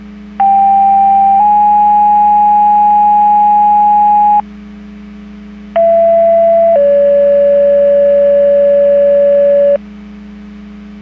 Two-tone page, heard as a one-second tone, then a three-second tone, about 790 then 830 Hz, about 670 then 570 Hz. It is heard from 0.4 to 9.8 s.
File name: Portage / 03208 15-55 Portage 154.385 Paging